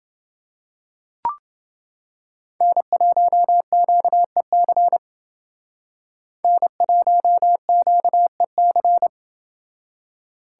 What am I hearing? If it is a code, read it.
Morse code, heard as a keyed tone, about 29 wpm, 700 Hz. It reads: N1QEC N1QEC